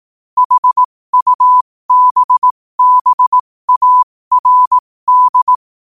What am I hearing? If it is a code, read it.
Morse code, heard as a keyed tone, about 18 wpm, spaced as for 15 wpm, 1000 Hz. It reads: HUBBARD
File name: qotdfiles / 15wpm